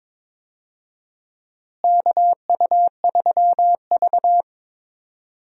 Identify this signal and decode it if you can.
Morse code, heard as a keyed tone, about 22 wpm, 700 Hz. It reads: KU3V
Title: USA Call Signs - 4 Characters - Repeated 2x - 22wpm